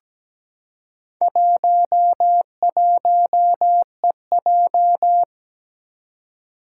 Morse code, heard as a keyed tone, about 17 wpm, 700 Hz. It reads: 11EJ
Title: Four Challenging Letter-Number Combinations 17wpm